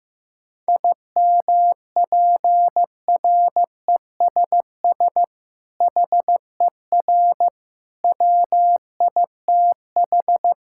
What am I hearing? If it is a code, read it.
Morse code, heard as a keyed tone, about 15 wpm, 700 Hz. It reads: IMPRESS HER WITH
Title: Sentences from Top 1600 Words 15wpm